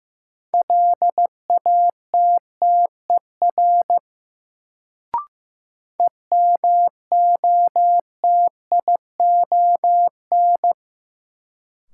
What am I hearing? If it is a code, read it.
Morse code, heard as a keyed tone, about 15 wpm, 700 Hz. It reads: LATTER EMOTION
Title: Sentences from Top 1600 Words - New Word Drill - 15wpm